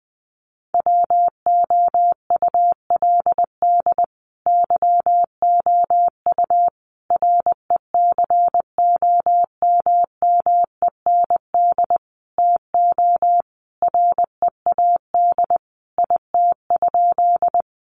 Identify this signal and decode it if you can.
Morse code, heard as a keyed tone, about 20 wpm, 700 Hz. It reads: WOULD YOU RECOMMEND TO READ IT?
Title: Sentences from Top 1200 Words 20wpm